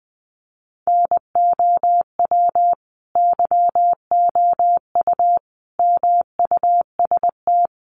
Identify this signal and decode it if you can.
Morse code, heard as a keyed tone, about 20 wpm, 700 Hz. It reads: NOW YOU MUST
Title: Sentences from Top 300 Words 20wpm